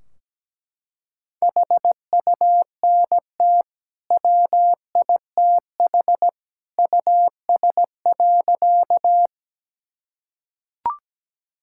Morse code, heard as a keyed tone, about 17 wpm, 700 Hz. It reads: HUNT WITH US.
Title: Sentences from Top 1600 Words 17wpm